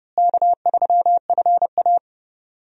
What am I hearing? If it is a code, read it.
Morse code, heard as a keyed tone, about 30 wpm, 700 Hz. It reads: K3FA
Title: USA Call Signs - 4 Characters - Repeated 3x - 30wpm